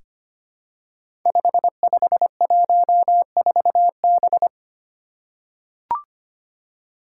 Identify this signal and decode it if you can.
Morse code, heard as a keyed tone, about 25 wpm, 700 Hz. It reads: 5514B
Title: Five Challenging Letter-Number Combinations 25wpm